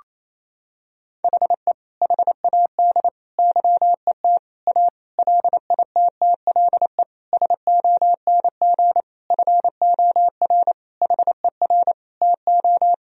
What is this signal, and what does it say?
Morse code, heard as a keyed tone, about 28 wpm, 700 Hz. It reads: HE HAD YET A LITTLE SONG FOR HER TO
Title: Sentences from Top 1000 Words - Review Part 2 - 28wpm